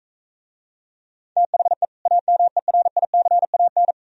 Morse code, heard as a keyed tone, about 43 wpm, 700 Hz. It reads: THE AMERICAN